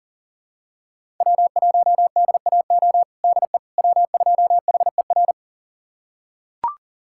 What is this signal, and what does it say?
Morse code, heard as a keyed tone, about 40 wpm, 700 Hz. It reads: W1DAO DE W2HER
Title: USA Callsign DE Callsign - 5 Characters - 40wpm